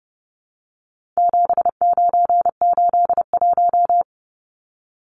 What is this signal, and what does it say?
Morse code, heard as a keyed tone, about 30 wpm, 700 Hz. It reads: 7981
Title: Numbers - 4-Digit 30wpm